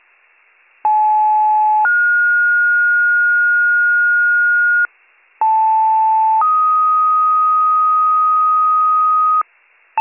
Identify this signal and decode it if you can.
Two-tone page, heard as a one-second tone, then a three-second tone, about 860 then 1430 Hz, about 880 then 1250 Hz, with noise. It is heard from 0.8 to 9.4 s.